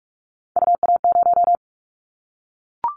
Morse code, heard as a keyed tone, about 44 wpm, 700 Hz. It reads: UA0
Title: Callsign Prefixes for Top 25 DXCC - Repeated 3x - 45wpm